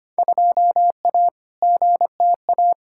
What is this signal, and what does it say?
Morse code, heard as a keyed tone, about 25 wpm, 700 Hz. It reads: 2A GTA
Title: Field Day Exchanges Encore - 25wpm